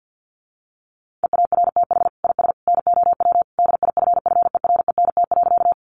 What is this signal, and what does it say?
Morse code, heard as a keyed tone, about 50 wpm, 700 Hz. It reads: EARTH IS NOW DIFFERENT.
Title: Sentences from Top 600 Words 50wpm